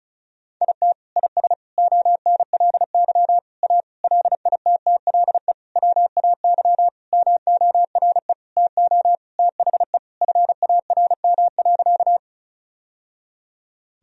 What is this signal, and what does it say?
Morse code, heard as a keyed tone, about 34 wpm, 700 Hz. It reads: IT IS ONLY A LITTLE WAY MORE TO THE FARM.
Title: Sentences from Top 1000 Words - Review Part 2 - 35wpm